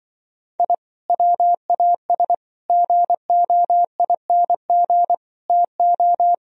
Morse code, heard as a keyed tone, about 25 wpm, 700 Hz. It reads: I WAS GOING TO
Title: Speed-Racing - Sets of 5 Words 20wpm